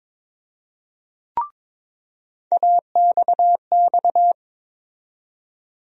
Morse code, heard as a keyed tone, about 22 wpm, 700 Hz. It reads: AXX
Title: USA Call Sign Postfixes - 3 Letters 22wpm